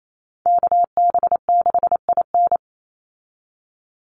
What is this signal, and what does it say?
Morse code, heard as a keyed tone, about 28 wpm, 700 Hz. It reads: KB6IN
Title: USA Call Signs 28wpm